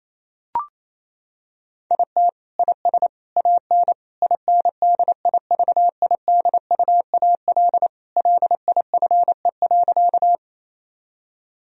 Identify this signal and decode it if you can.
Morse code, heard as a keyed tone, about 28 wpm, 700 Hz. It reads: IT IS AN INDIVIDUAL LIFE.
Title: Sentences from Top 1000 Words - Review Part 4 - 28wpm